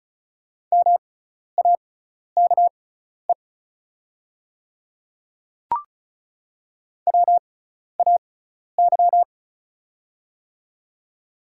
Morse code, heard as a keyed tone, about 34 wpm, spaced as for 12 wpm, 690 Hz. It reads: MAKE WAY
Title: 6x Character Spacing - Top 100 Words - 35wpm